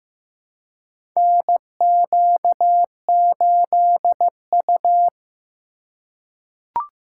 Morse code, heard as a keyed tone, about 15 wpm, 700 Hz. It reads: NQ8U
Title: USA Call Signs - 3x Character Spacing - 15wpm